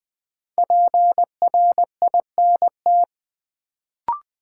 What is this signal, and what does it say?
Morse code, heard as a keyed tone, about 20 wpm, 700 Hz. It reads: PRINT
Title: Sentences from Top 1400 Words - New Word Drill Alt - 20wpm